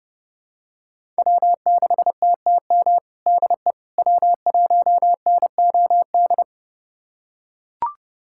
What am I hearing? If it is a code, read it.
Morse code, heard as a keyed tone, about 30 wpm, 700 Hz. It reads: W6TTM DE W1NOD
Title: USA Callsign DE Callsign - 5 Characters - 30wpm